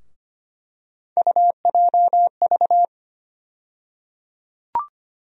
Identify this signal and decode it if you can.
Morse code, heard as a keyed tone, about 25 wpm, 700 Hz. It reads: UJV
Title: USA Call Sign Postfixes - 3 Letters 25wpm